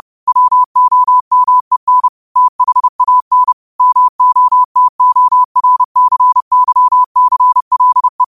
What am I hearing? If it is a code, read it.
Morse code, heard as a keyed tone, about 30 wpm, 1000 Hz. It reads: WOMEN THAN MOTORCYCLE
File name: qotdfiles / 30wpm